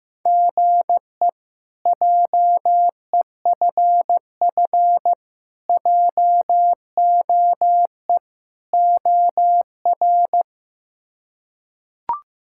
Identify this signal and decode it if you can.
Morse code, heard as a keyed tone, about 16 wpm, 700 Hz. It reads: GE JEFF JOE OR
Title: K1USN SST QSO Practice - 15wpm